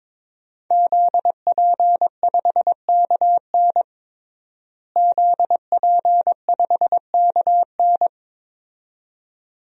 Morse code, heard as a keyed tone, about 22 wpm, 700 Hz. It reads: ZP5KN ZP5KN
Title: DXCC-100 Call Signs - 5 Characters - Repeated 2x - 22wpm